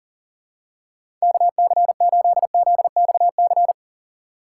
Morse code, heard as a keyed tone, about 38 wpm, 700 Hz. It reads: KC8ZXC
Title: USA Call Signs - 4x Character Spacing - 40wpm